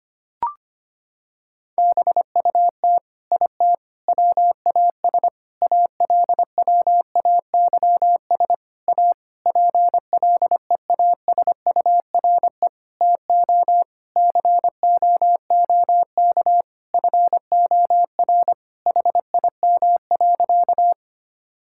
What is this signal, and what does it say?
Morse code, heard as a keyed tone, about 25 wpm, 700 Hz. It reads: BUT IT WAS ALWAYS A PLEASURE TO COOK FOR HIM.
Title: Sentences from Top 1600 Words 25wpm